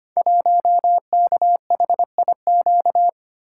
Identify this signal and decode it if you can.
Morse code, heard as a keyed tone, about 25 wpm, 700 Hz. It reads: 1KHIQ
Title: Code Groups - 5 Letter:Numbers - 25wpm